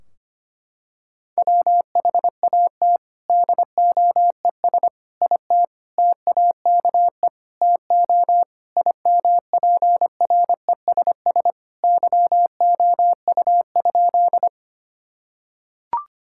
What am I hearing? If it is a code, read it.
Morse code, heard as a keyed tone, about 25 wpm, 700 Hz. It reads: WHAT DOES IT TAKE TO IMPRESS YOU?
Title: Sentences from Top 1600 Words 25wpm